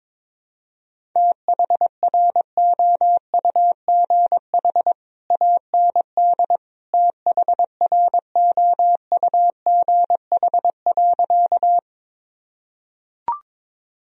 Morse code, heard as a keyed tone, about 22 wpm, 700 Hz. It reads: THROUGH AND THROUGH.
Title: Sentences from Top 1000 Words - Review Part 1 - 22wpm